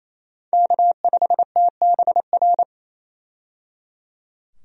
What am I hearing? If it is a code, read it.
Morse code, heard as a keyed tone, about 27 wpm, 700 Hz. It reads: K5TBR
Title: USA Call Signs - 5 Characters - 28wpm